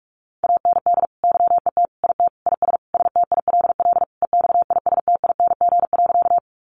Morse code, heard as a keyed tone, about 44 wpm, 700 Hz. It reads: AND YET IT IS STILL EXISTING.
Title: Sentences from Top 1500 Words 45wpm